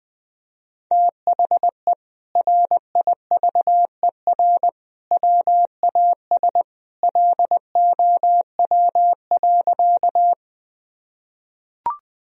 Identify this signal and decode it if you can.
Morse code, heard as a keyed tone, about 20 wpm, 700 Hz. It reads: THE RIVER WAS LOW.
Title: Sentences from Top 1000 Words - Review Part 1 - 20wpm